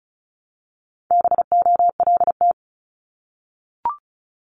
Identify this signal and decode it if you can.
Morse code, heard as a keyed tone, about 36 wpm, 700 Hz. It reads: BOLT
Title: Multi-letter Phonemes Repeated 3x 35wpm